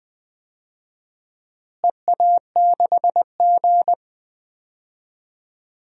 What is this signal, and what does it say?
Morse code, heard as a keyed tone, about 20 wpm, 700 Hz. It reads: EA6G
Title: DXCC-100 Call Signs - 4 Characters - Repeated 2x - 20wpm